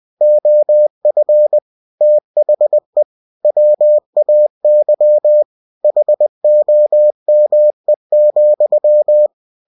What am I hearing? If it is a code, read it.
Morse code, heard as a keyed tone, about 20 wpm, 600 Hz. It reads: OF THE WAY HOME,